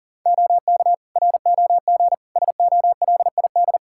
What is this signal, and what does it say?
Morse code, heard as a keyed tone, about 40 wpm, 700 Hz. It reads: OK ROG SOLID
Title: Practice QSO - Exchange 3 - More Info - 40wpm